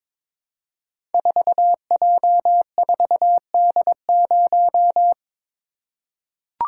Morse code, heard as a keyed tone, about 22 wpm, 700 Hz. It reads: 4J4D0
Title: Code Groups - 5 Letter:Numbers - 22wpm